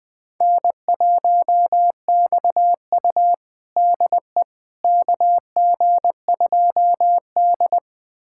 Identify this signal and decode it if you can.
Morse code, heard as a keyed tone, about 21 wpm, 700 Hz. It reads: N1XU DE KG2D
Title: USA Callsign DE Callsign - 4 Characters - 20wpm